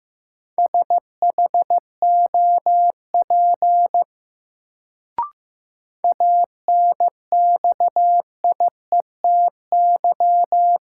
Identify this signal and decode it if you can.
Morse code, heard as a keyed tone, about 15 wpm, 700 Hz. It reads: SHOP ANXIETY
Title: Sentences from Top 1600 Words - New Word Drill - 15wpm